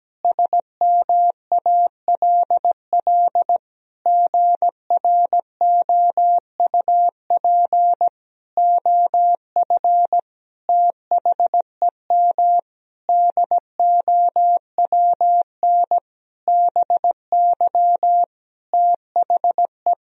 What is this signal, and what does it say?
Morse code, heard as a keyed tone, about 17 wpm, 700 Hz. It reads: SMALL GROUP OF THEM DOWN BY THE
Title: Sentences from Top 1000 Words - Review Part 1 - 17wpm